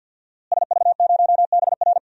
Morse code, heard as a keyed tone, about 52 wpm, 700 Hz. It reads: SV0BR
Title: DXCC-100 Call Signs - 5 Characters - Repeated 3x - 50wpm